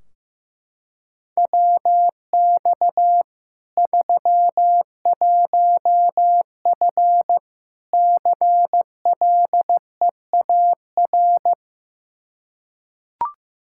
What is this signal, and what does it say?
Morse code, heard as a keyed tone, about 15 wpm, 700 Hz. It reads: WX 31F CLEAR
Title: Practice QSO - Exchange 2 - Info - 15wpm